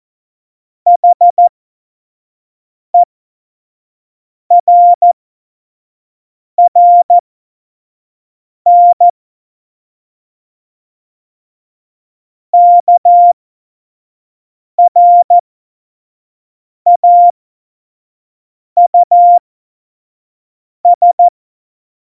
Morse code, heard as a keyed tone, about 14 wpm, spaced as for 5 wpm, 700 Hz. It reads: HERRN KRAUS
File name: 25 F70